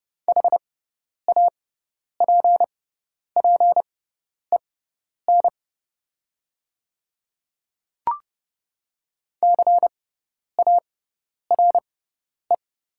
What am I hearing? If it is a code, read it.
Morse code, heard as a keyed tone, about 30 wpm, spaced as for 10 wpm, 690 Hz. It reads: HAPPEN CARE
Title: 6x Character Spacing - Top 500 Words - 30wpm